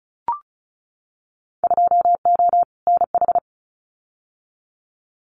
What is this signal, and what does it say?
Morse code, heard as a keyed tone, about 34 wpm, 700 Hz. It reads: 2O NH
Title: Winter Field Day Exchanges 35wpm